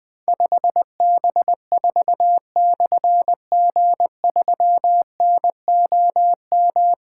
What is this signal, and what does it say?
Morse code, heard as a keyed tone, about 20 wpm, 700 Hz. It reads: 5B4/G3NOM
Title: Difficult Call Signs 20wpm